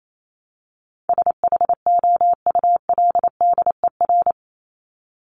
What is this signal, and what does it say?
Morse code, heard as a keyed tone, about 28 wpm, 700 Hz. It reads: SHOULDER
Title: Top 1501-1750 Words 28wpm